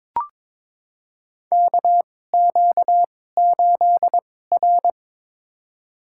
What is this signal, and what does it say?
Morse code, heard as a keyed tone, about 22 wpm, spaced as for 16 wpm, 700 Hz. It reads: KQ8R
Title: USA Call Signs - 4 Characters - 2x Character Spacing - 22wpm